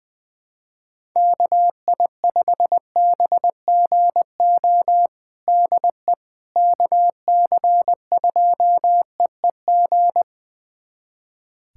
Morse code, heard as a keyed tone, about 20 wpm, 700 Hz. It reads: KI5BGO DE KC2EEG
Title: USA Callsign DE Callsign 20wpm